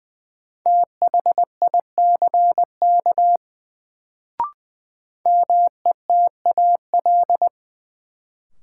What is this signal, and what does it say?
Morse code, heard as a keyed tone, about 20 wpm, 700 Hz. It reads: THICK METAL
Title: Sentences from Top 1600 Words - New Word Drill Alt - 20wpm